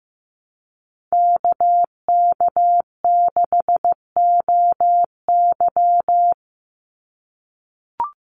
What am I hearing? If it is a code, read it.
Morse code, heard as a keyed tone, about 14 wpm, 700 Hz. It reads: KK6OY
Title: USA Call Signs 15wpm